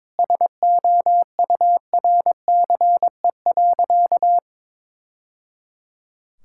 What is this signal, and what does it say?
Morse code, heard as a keyed tone, about 22 wpm, 700 Hz. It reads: SOURCE.
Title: Sentences from Top 1000 Words - Review Part 5 - 22wpm